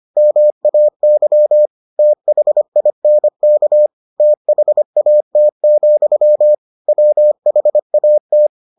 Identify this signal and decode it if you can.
Morse code, heard as a keyed tone, about 25 wpm, 600 Hz. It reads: MAY THINK THAT, WHAT